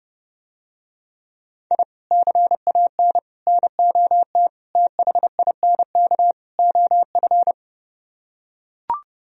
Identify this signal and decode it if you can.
Morse code, heard as a keyed tone, about 31 wpm, 700 Hz. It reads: I CAN NOT THINK OF